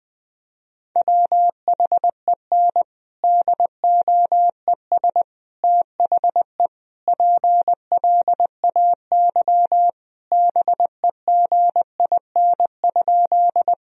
Morse code, heard as a keyed tone, about 20 wpm, 700 Hz. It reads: WHEN DOES THE PLAY BEGIN?
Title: Sentences from Top 1000 Words - Review Part 1 - 20wpm